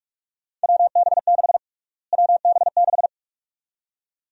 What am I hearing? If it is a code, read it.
Morse code, heard as a keyed tone, about 47 wpm, 700 Hz. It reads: WB6 WB6
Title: USA Call Sign Prefixes - Repeated 2x - 45wpm